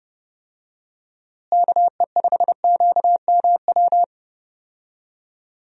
Morse code, heard as a keyed tone, about 30 wpm, 700 Hz. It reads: KE5QMW
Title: USA Call Signs - 6 Characters - Repeated 3x - 30wpm